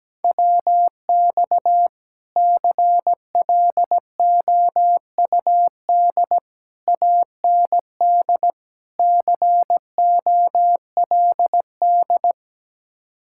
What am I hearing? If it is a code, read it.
Morse code, heard as a keyed tone, about 17 wpm, 700 Hz. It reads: WX CLOUD AND COLD
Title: Speed-Racing Common QSO Phrases - 17wpm